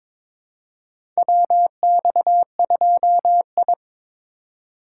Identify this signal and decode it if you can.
Morse code, heard as a keyed tone, about 22 wpm, 700 Hz. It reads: WX2I
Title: USA Call Signs - 4 Characters - 4x Character Spacing - 22wpm